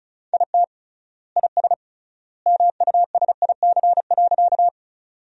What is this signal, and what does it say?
Morse code, heard as a keyed tone, about 34 wpm, 700 Hz. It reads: IT IS MUSIC.
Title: 3x Word-Spacing - Sentences from Top 300 Words - 35wpm